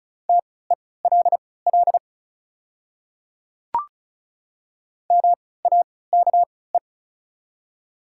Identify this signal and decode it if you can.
Morse code, heard as a keyed tone, about 36 wpm, spaced as for 20 wpm, 690 Hz. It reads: TELL MAKE
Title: 3x Character Spacing - Top 200 Words - 35wpm